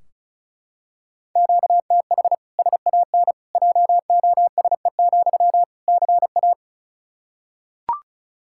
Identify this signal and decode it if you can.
Morse code, heard as a keyed tone, about 36 wpm, 700 Hz. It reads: QTH SAN JOSE, CA
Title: Practice QSO - Exchange 1 - Beginning - 35wpm